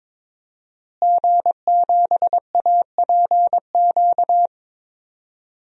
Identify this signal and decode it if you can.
Morse code, heard as a keyed tone, about 21 wpm, 700 Hz. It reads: G7APQ
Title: DXCC-100 Call Signs - 5 Characters - Repeated 3x - 22wpm